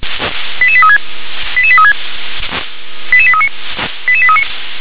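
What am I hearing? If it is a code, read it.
Five-tone selective call (ZVEI), heard as quick runs of five tones, tones of about 70 ms. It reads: 88836 88836 88839 88839
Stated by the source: ZVEI five-tone call received by SDR.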